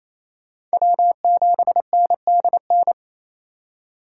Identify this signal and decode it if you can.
Morse code, heard as a keyed tone, about 36 wpm, 700 Hz. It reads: W7NDN